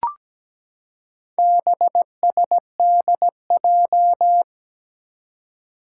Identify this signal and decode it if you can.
Morse code, heard as a keyed tone, about 17 wpm, 700 Hz. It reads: BSDJ
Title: Four Challenging Letter-Number Combinations 17wpm